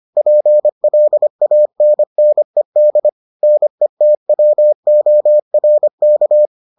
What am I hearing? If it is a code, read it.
Morse code, heard as a keyed tone, about 25 wpm, 600 Hz. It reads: PLANNED NETWORK